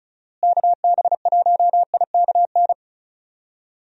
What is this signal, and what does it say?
Morse code, heard as a keyed tone, about 36 wpm, 700 Hz. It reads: KB1IKN
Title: USA Call Signs - 6 Characters - Repeated 2x - 35wpm